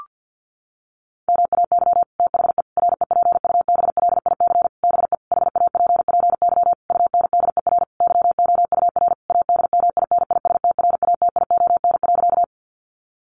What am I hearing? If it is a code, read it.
Morse code, heard as a keyed tone, about 50 wpm, 700 Hz. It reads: MAY THE REPUBLIC BE HAPPY UNDER YOUR ADMINISTRATION.
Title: Sentences from Top 1500 Words 50wpm